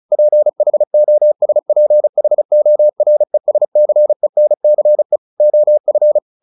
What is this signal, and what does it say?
Morse code, heard as a keyed tone, about 34 wpm, 600 Hz. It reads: PHOSPHORESCENCE OF
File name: the old man and the sea / the old man and the sea 1 35wpm